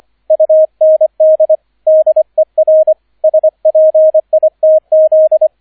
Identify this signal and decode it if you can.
Morse code, heard as a keyed tone, about 25 wpm, 630 Hz, with radio noise. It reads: UND DER SPITZ